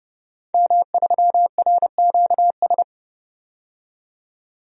Morse code, heard as a keyed tone, about 30 wpm, 700 Hz. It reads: M3RQS
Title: DXCC-100 Call Signs - 5 Characters - 30wpm